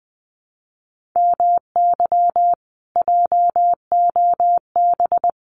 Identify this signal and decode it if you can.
Morse code, heard as a keyed tone, about 20 wpm, 700 Hz. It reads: MY JOB
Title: Common QSO Phrases Encore 20wpm